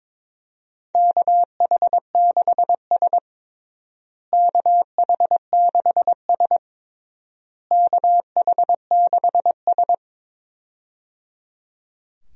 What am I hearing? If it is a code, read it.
Morse code, heard as a keyed tone, about 22 wpm, 700 Hz. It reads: KH6S KH6S KH6S
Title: DXCC-100 Call Signs - 4 Characters - Repeated 3x - 22wpm